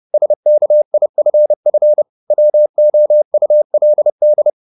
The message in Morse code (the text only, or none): SKIFF WOULD